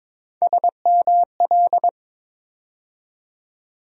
Morse code, heard as a keyed tone, about 22 wpm, 700 Hz. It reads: SML
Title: USA Call Sign Postfixes - 3 Letters 22wpm